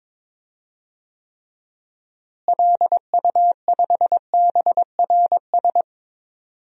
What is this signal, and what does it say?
Morse code, heard as a keyed tone, about 21 wpm, 700 Hz. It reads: LU5BRS
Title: DXCC-100 Call Signs - 6 Characters - Repeated 3x - 22wpm